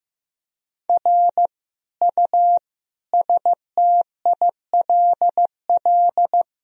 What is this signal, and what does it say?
Morse code, heard as a keyed tone, about 15 wpm, 700 Hz. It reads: R U STILL
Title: Sentences from Top 400 Words and Abbreviations 15wpm